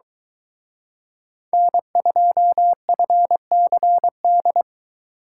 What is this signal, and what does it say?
Morse code, heard as a keyed tone, about 23 wpm, 700 Hz. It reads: N2FCD